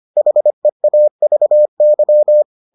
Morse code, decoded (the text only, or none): HEAVY